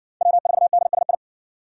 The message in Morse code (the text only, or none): W4DSI